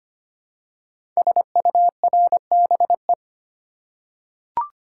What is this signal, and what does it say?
Morse code, heard as a keyed tone, about 25 wpm, 700 Hz. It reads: SURBE
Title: Code Groups - 5 Letters - 25wpm